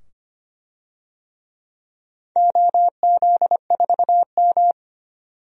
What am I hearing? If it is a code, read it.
Morse code, heard as a keyed tone, about 25 wpm, 700 Hz. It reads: OZ4M
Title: DXCC-100 Call Signs - 4 Characters - 25wpm